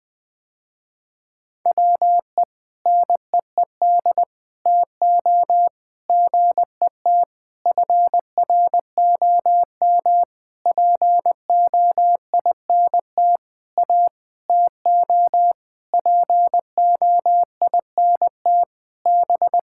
Morse code, heard as a keyed tone, about 19 wpm, 700 Hz. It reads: WE NEED TO GET FROM POINT A TO POINT B
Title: Sentences with 6-Word Phrases - Spoken Beginning - 20wpm